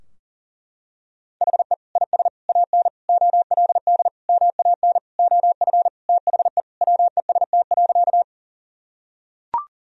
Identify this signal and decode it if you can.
Morse code, heard as a keyed tone, about 40 wpm, 700 Hz. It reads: HE IS AN OLD MAN OF THE WEST.
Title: Sentences from Top 1000 Words - Review Part 2 - 40wpm